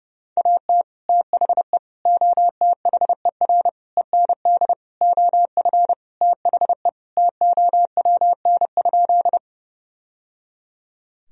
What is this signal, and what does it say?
Morse code, heard as a keyed tone, about 30 wpm, 700 Hz. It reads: AT THE OTHER END OF THE TOWN?
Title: Sentences from Top 1000 Words - Review Part 2 - 30wpm